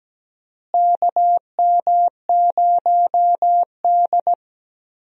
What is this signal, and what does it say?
Morse code, heard as a keyed tone, about 17 wpm, 700 Hz. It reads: KM0D
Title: USA Call Signs - 4 Characters - Repeated 3x - 17wpm